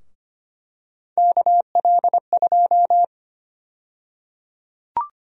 Morse code, heard as a keyed tone, about 25 wpm, 700 Hz. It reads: KL2
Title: USA Call Sign Prefixes - 25wpm